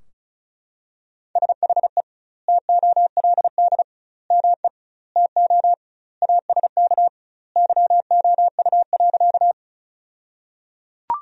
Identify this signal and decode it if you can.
Morse code, heard as a keyed tone, about 36 wpm, 700 Hz. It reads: SHE TOLD ME TO ASK YOU.
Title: 2x Word-Spacing - Sentences from Top 400 Words - 35wpm